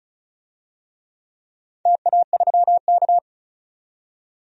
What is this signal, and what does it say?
Morse code, heard as a keyed tone, about 36 wpm, 700 Hz. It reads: TA3K
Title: DXCC-100 Call Signs - 4 Characters - 35wpm